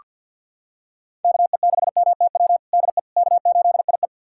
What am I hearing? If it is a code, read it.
Morse code, heard as a keyed tone, about 49 wpm, 700 Hz. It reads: KE6MTW DE K8IE
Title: USA Callsign DE Callsign 50wpm